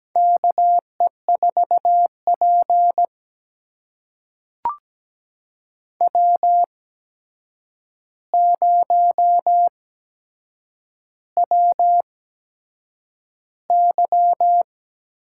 Morse code, heard as a keyed tone, about 17 wpm, spaced as for 7 wpm, 700 Hz. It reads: KE4P W0WY
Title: USA Call Signs - 4 Characters - 8x Character Spacing - 17wpm